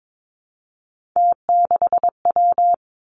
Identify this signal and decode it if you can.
Morse code, heard as a keyed tone, about 22 wpm, 700 Hz. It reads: T6W
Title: Three Challenging Letter-Number Combinations 22wpm